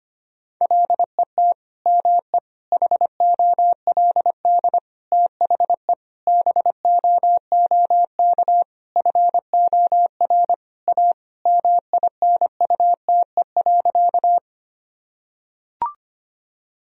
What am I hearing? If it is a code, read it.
Morse code, heard as a keyed tone, about 25 wpm, 700 Hz. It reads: LET ME HOLD THE BOOK FOR A MINUTE.
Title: Sentences from Top 1000 Words - Review Part 2 - 25wpm